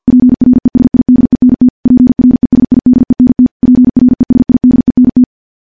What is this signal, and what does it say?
Morse code, heard as a keyed tone, about 51 wpm, 250 Hz. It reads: PRESIDENT PRESIDENT PRESIDENT
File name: morse / morse-training-50WPM